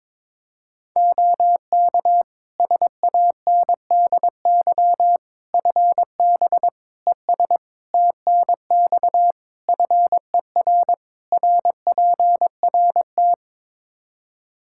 Morse code, heard as a keyed tone, about 22 wpm, 700 Hz. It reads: OK SANDY FB ES TNX FER RPRT
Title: K1USN SST QSO Practice 22wpm